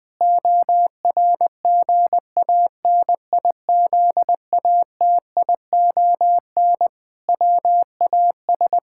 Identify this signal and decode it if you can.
Morse code, heard as a keyed tone, about 20 wpm, 700 Hz. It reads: ORGANIZATION WAS
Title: Sentences from Top 1000 Words - Review Part 5 - 20wpm